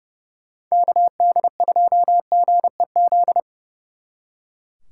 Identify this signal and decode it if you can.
Morse code, heard as a keyed tone, about 36 wpm, 700 Hz. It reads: KD2GEZ
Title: Speed Racing - USA 2x3 Call Signs 30wpm